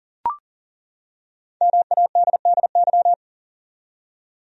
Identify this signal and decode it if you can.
Morse code, heard as a keyed tone, about 40 wpm, 700 Hz. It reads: MADDY
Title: USA Common Names - 40wpm